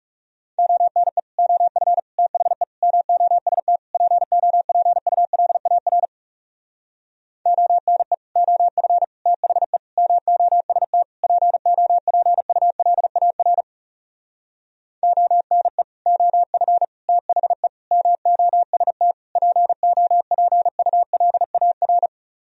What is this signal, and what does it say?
Morse code, heard as a keyed tone, about 40 wpm, 700 Hz. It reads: ONE OF THE MOST POPULAR ONE OF THE MOST POPULAR ONE OF THE MOST POPULAR